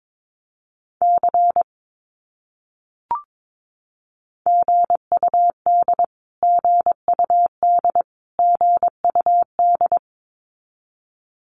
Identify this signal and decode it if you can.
Morse code, heard as a keyed tone, about 22 wpm, 700 Hz. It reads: C GUD GUD GUD
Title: Abbreviations Repeated 3x - 22wpm